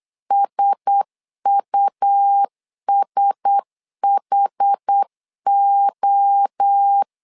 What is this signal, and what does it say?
Morse code, heard as a keyed tone, about 8 wpm, 800 Hz. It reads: SUSHO
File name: LEZIONI / 4 Q1S 8wpm 2710